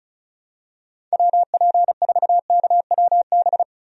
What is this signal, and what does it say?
Morse code, heard as a keyed tone, about 34 wpm, 700 Hz. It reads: WP4KWB
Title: USA Call Signs - 6 Characters - Repeated 3x - 35wpm